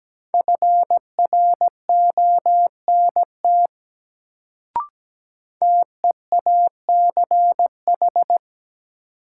Morse code, heard as a keyed tone, about 17 wpm, 700 Hz. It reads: FRONT TEACH
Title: Sentences from Top 500 Words - New Word Drill - 17wpm